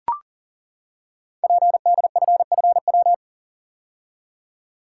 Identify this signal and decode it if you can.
Morse code, heard as a keyed tone, about 40 wpm, 700 Hz. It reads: PDFFW
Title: Code Groups - 5 Letter:Numbers - 40wpm